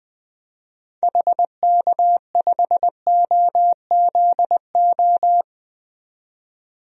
Morse code, heard as a keyed tone, about 20 wpm, 700 Hz. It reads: HK5OZO
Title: DXCC-100 Call Signs - 6 Characters - 20wpm